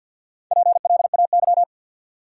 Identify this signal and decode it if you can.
Morse code, heard as a keyed tone, about 53 wpm, 700 Hz. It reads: PLAY
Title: Sentences from Top 200 Words - New Word Drill - 50wpm